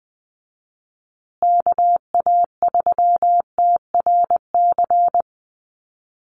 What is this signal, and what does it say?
Morse code, heard as a keyed tone, about 20 wpm, 700 Hz. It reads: KA3TRC
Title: USA Call Signs Repeated 3x 20wpm